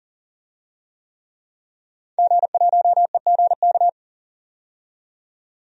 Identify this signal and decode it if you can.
Morse code, heard as a keyed tone, about 40 wpm, 700 Hz. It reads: G1EGK
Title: DXCC-100 Call Signs - 5 Characters - 40wpm